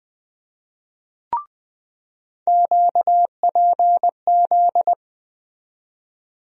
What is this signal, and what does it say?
Morse code, heard as a keyed tone, about 20 wpm, 700 Hz. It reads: QPZ
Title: USA Call Sign Postfixes - 3 Letters 20wpm